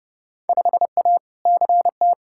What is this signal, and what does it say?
Morse code, heard as a keyed tone, about 32 wpm, 700 Hz. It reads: 5A CT
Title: Field Day Exchanges - 30wpm